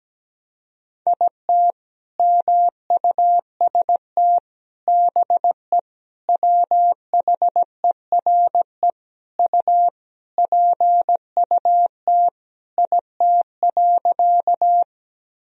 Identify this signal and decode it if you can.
Morse code, heard as a keyed tone, about 17 wpm, 700 Hz. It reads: IT MUST BE WHERE U PUT IT.
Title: Sentences from Top 200 Words and Abbreviations 17wpm